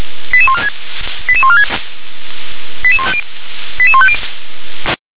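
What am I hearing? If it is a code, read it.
Five-tone selective call (ZVEI), heard as quick runs of five tones, tones of about 70 ms. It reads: 88146 88146 88150
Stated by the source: ZVEI five-tone selective call 88154, received with an SDR.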